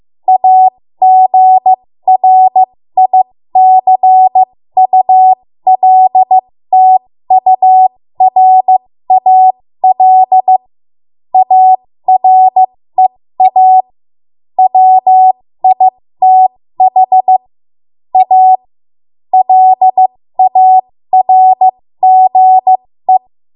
Morse code, heard as a keyed tone, about 15 wpm, spaced as for 13 wpm, 750 Hz. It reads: AGRICULTURAL AREA WITH A LARGE
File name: Morse / 13WPM